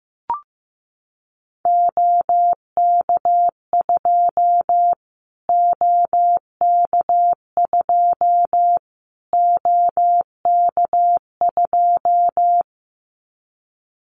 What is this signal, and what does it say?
Morse code, heard as a keyed tone, about 15 wpm, 700 Hz. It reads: OK2 OK2 OK2
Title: Callsign Prefixes for Top 25 DXCC - Repeated 3x - 15wpm